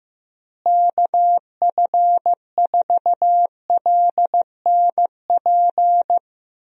Morse code, heard as a keyed tone, about 15 wpm, 700 Hz. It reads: KF4LNP
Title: USA Call Signs - 6 Characters - Repeated 3x - 15wpm